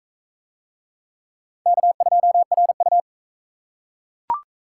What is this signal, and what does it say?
Morse code, heard as a keyed tone, about 44 wpm, 700 Hz. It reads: K2RU